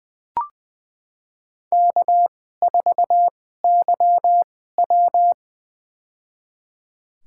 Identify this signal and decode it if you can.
Morse code, heard as a keyed tone, about 20 wpm, spaced as for 14 wpm, 700 Hz. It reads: K4YW
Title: USA Call Signs - 4 Characters - 2x Character Spacing - 20wpm